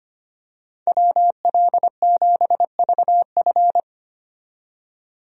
Morse code, heard as a keyed tone, about 25 wpm, 690 Hz. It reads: WL7VF
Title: USA Call Signs - 5 Characters - 25wpm